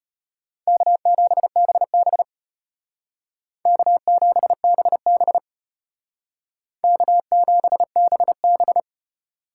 Morse code, heard as a keyed tone, about 31 wpm, 700 Hz. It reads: K7BB K7BB K7BB
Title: Speed Racing - USA 1x2 Call Signs 25wpm